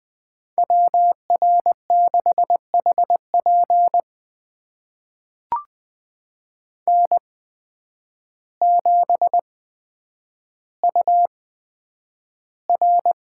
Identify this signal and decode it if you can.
Morse code, heard as a keyed tone, about 20 wpm, spaced as for 9 wpm, 700 Hz. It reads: WR6HP N7UR
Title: USA Call Signs - 5 Characters - 8x Character Spacing - 20wpm